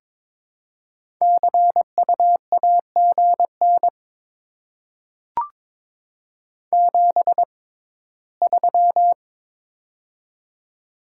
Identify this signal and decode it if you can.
Morse code, heard as a keyed tone, about 22 wpm, spaced as for 15 wpm, 700 Hz. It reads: CU AGN 73
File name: Top 20 QSO Elements - 6x Character Spacing - 22wpm